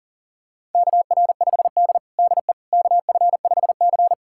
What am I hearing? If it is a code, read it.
Morse code, heard as a keyed tone, about 40 wpm, 700 Hz. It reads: KR5D DE KF5C
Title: USA Callsign DE Callsign - 4 Characters - 40wpm